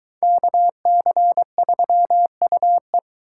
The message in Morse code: KC3UE

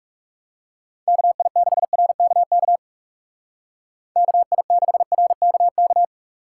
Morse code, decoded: KI6RKK KI6RKK

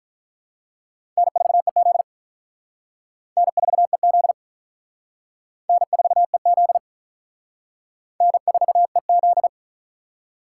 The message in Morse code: N4EZ N4EZ N4EZ N4EZ

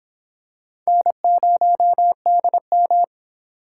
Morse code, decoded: N0DM

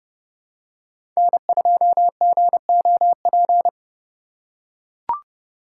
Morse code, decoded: N2GOP

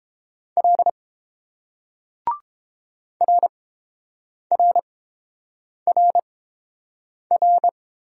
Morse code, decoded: L RRRR

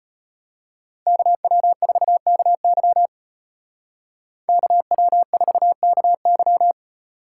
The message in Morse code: KW4KY KW4KY

